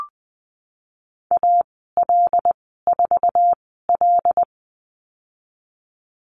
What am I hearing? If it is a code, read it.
Morse code, heard as a keyed tone, about 20 wpm, spaced as for 14 wpm, 700 Hz. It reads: AL4L
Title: 082 Lesson 14g - Number 2 - Call Signs 20wpm